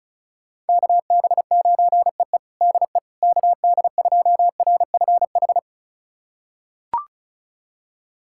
Morse code, heard as a keyed tone, about 36 wpm, 700 Hz. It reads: KB9EE DE KD2RFH
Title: USA Callsign DE Callsign 35wpm